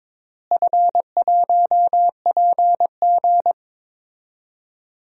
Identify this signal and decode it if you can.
Morse code, heard as a keyed tone, about 22 wpm, 700 Hz. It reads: F1PG
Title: DXCC-100 Call Signs - 4 Characters - Repeated 2x - 22wpm